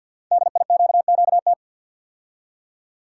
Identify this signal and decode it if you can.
Morse code, heard as a keyed tone, about 50 wpm, 700 Hz. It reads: DIQQT
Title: Code Groups - 5 Letters - 50wpm